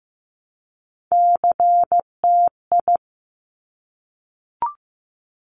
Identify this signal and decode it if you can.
Morse code, heard as a keyed tone, about 15 wpm, 700 Hz. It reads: CTI
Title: Top Three Letter Combinations in Words 15wpm